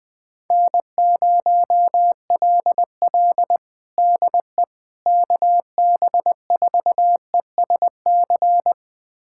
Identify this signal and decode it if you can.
Morse code, heard as a keyed tone, about 20 wpm, 690 Hz. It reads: N0LL DE KB4ESC
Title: USA Callsign DE Callsign 20wpm